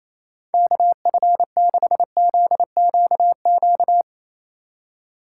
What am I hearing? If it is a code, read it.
Morse code, heard as a keyed tone, about 35 wpm, 700 Hz. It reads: KF6ZQQ